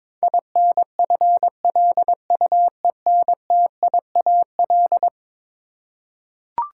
Morse code, heard as a keyed tone, about 22 wpm, 700 Hz. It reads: INFLUENTIAL
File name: practice / 22wpm / Eleven Letter Words - 22wpm